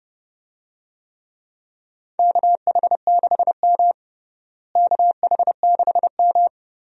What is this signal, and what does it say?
Morse code, heard as a keyed tone, about 30 wpm, 700 Hz. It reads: KH6M KH6M
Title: USA Call Signs - 4 Characters - Repeated 2x - 30wpm